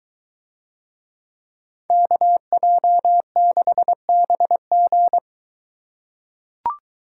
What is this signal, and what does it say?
Morse code, heard as a keyed tone, about 23 wpm, 700 Hz. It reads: KJ6BG